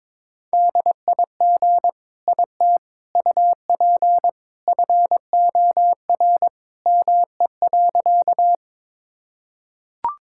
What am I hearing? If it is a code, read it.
Morse code, heard as a keyed tone, about 22 wpm, 700 Hz. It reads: DIG IT UP FOR ME.
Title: Sentences from Top 1600 Words 22wpm